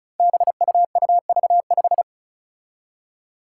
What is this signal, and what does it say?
Morse code, heard as a keyed tone, about 36 wpm, 700 Hz. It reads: BUUV5
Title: Five Challenging Letter-Number Combinations 35wpm